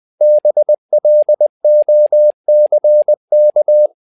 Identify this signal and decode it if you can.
Morse code, heard as a keyed tone, about 20 wpm, 600 Hz. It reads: BLOCK
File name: tower trouble 20wpm